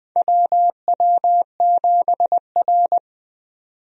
Morse code, heard as a keyed tone, about 20 wpm, 700 Hz. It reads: WW7R
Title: USA Call Signs - 4 Characters - 20wpm